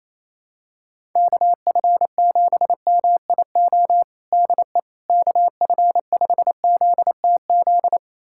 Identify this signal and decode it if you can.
Morse code, heard as a keyed tone, about 28 wpm, 700 Hz. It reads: KF7MIO DE KF5ZTZ
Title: USA Callsign DE Callsign 28wpm